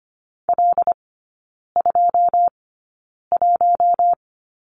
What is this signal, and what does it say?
Morse code, heard as a keyed tone, about 25 wpm, spaced as for 9 wpm, 690 Hz. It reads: L21
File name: Lesson 14e - Number 2 - Three Characters 25wpm